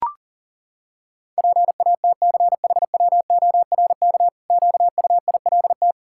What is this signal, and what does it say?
Morse code, heard as a keyed tone, about 37 wpm, 700 Hz. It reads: PATCHWORK QUILT
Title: Pangram Challenge v2 - 40wpm